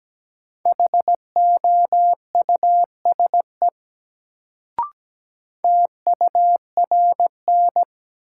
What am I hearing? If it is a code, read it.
Morse code, heard as a keyed tone, about 17 wpm, 700 Hz. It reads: HOUSE TURN
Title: Sentences from Top 200 Words - New Word Drill - 17wpm